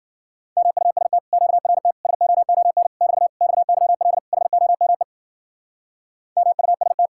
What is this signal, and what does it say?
Morse code, heard as a keyed tone, about 51 wpm, 700 Hz. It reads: MUST QRT SOON <BT> XYL HOME MUST